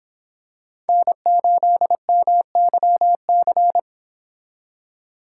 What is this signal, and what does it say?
Morse code, heard as a keyed tone, about 26 wpm, 700 Hz. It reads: N8MYC